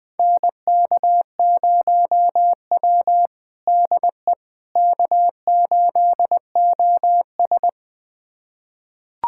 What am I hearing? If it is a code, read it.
Morse code, heard as a keyed tone, about 21 wpm, 700 Hz. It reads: NK0W DE K8OS
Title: USA Callsign DE Callsign - 4 Characters - 20wpm